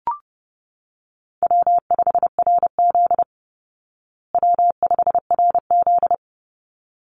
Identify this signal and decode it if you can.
Morse code, heard as a keyed tone, about 30 wpm, 700 Hz. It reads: W5RZ W5RZ
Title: Easy Call Signs Repeated 2x - 30wpm